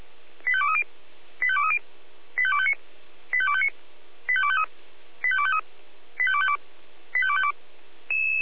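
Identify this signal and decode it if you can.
Five-tone selective call (ZVEI), heard as quick runs of five tones, tones of about 70 ms. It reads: 85230 85230 85259 85259 85253 85253 85252 85252
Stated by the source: Five-tone ZVEI selective call 85267, received by SDR.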